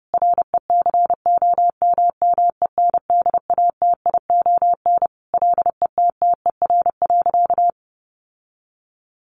Morse code, heard as a keyed tone, about 30 wpm, 700 Hz. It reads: RECOMMENDATION LETTER.
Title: Sentences from Top 1400 Words 30wpm